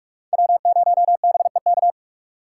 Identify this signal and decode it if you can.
Morse code, heard as a keyed tone, about 44 wpm, 700 Hz. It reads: W0BEK